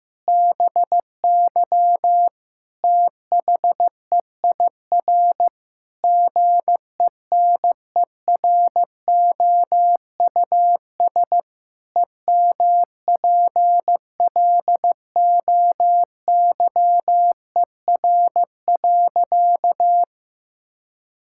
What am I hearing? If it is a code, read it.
Morse code, heard as a keyed tone, about 15 wpm, 700 Hz. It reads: BY THEIR GENEROUS EMPLOYER.
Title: Pangram Challenge v2 - 15wpm